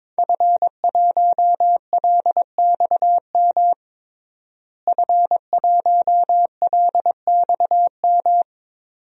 Morse code, heard as a keyed tone, about 22 wpm, 700 Hz. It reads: F1LXM F1LXM
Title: DXCC-100 Call Signs - 5 Characters - Repeated 2x - 22wpm